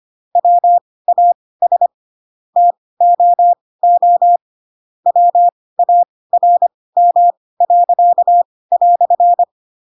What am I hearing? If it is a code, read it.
Morse code, heard as a keyed tone, about 25 wpm, spaced as for 18 wpm, 700 Hz. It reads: WAS TOO WARM."